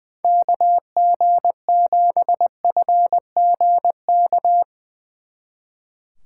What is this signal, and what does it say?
Morse code, heard as a keyed tone, about 22 wpm, 700 Hz. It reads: KG7FGK
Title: Speed Racing - USA 2x3 Call Signs 20wpm